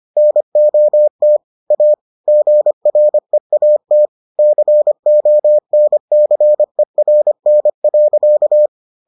Morse code, heard as a keyed tone, about 25 wpm, 600 Hz. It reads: NOT A GREAT CONCERN.